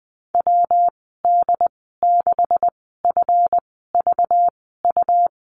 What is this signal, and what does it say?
Morse code, heard as a keyed tone, about 20 wpm, spaced as for 15 wpm, 700 Hz. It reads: WD6FVU
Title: 193 Lesson 31g - Letter v - Call Signs 20wpm